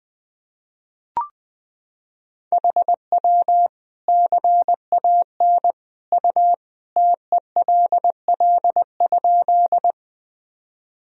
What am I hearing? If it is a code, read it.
Morse code, heard as a keyed tone, about 20 wpm, 700 Hz. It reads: HW CAN U TELL?
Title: Sentences from Top 200 Words and Abbreviations 20wpm